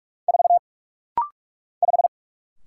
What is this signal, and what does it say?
Morse code, heard as a keyed tone, about 44 wpm, 700 Hz. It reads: 4 5
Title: Single Letter-Number - Rapid-Fire 45wpm